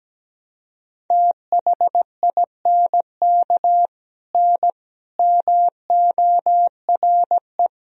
Morse code, heard as a keyed tone, about 17 wpm, 700 Hz. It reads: THINK N MORE
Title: Sentences from Top 200 Words and Abbreviations 17wpm